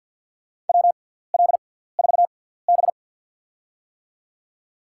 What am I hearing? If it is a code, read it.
Morse code, heard as a keyed tone, about 53 wpm, spaced as for 17 wpm, 700 Hz. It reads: WL4B
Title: USA Call Signs - 4 Characters - 6x Character Spacing - 50wpm